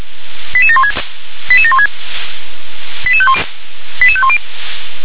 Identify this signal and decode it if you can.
Five-tone selective call (ZVEI), heard as quick runs of five tones, tones of about 80 ms. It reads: 88716 88716 88510